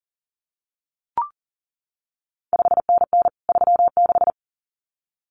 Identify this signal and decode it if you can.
Morse code, heard as a keyed tone, about 37 wpm, 700 Hz. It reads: 5NN 36
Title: CQ WW DX Contest Exchange Encore - 40wpm